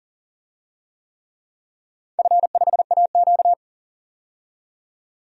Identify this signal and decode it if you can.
Morse code, heard as a keyed tone, about 40 wpm, 700 Hz. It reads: F5AQ
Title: DXCC-100 Call Signs - 4 Characters - 40wpm